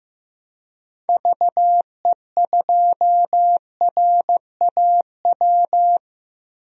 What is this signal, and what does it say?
Morse code, heard as a keyed tone, about 15 wpm, 700 Hz. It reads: VE2RAW
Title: DXCC-100 Call Signs - 6 Characters - Repeated 3x - 15wpm